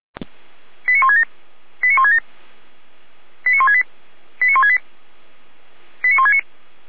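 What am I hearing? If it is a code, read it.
Five-tone selective call (ZVEI), heard as quick runs of five tones, tones of about 80 ms. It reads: 89167 89167 89168 89168 89169
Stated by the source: ZVEI five-tone call 89115 received by SDR.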